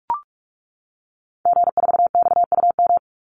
Morse code, heard as a keyed tone, about 44 wpm, 700 Hz. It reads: G4XUM
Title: Call Signs - 45wpm